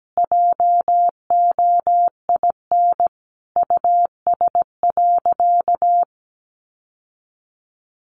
Morse code, heard as a keyed tone, about 17 wpm, 700 Hz. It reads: JOIN US.
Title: Sentences from Top 1400 Words 17wpm